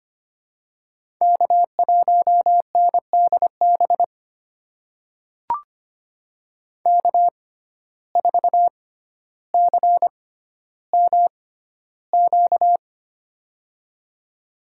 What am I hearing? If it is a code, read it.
Morse code, heard as a keyed tone, about 25 wpm, spaced as for 13 wpm, 700 Hz. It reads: K1NDB K4CMQ
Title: USA Call Signs - 5 Characters - 6x Character Spacing - 25wpm